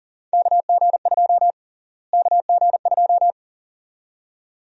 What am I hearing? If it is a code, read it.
Morse code, heard as a keyed tone, about 40 wpm, 700 Hz. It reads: KG2 KG2
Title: USA Call Sign Prefixes - Repeated 2x - 40wpm